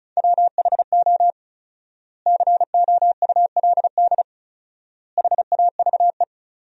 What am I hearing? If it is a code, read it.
Morse code, heard as a keyed tone, about 36 wpm, 700 Hz. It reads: WHO COULD HAVE
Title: 4x Word-Spacing - Sentences from Top 300 Words - 35wpm